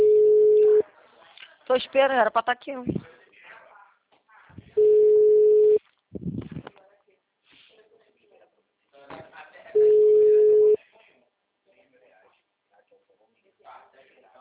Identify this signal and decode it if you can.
Ringback tone (Europe, 425 Hz), with speech.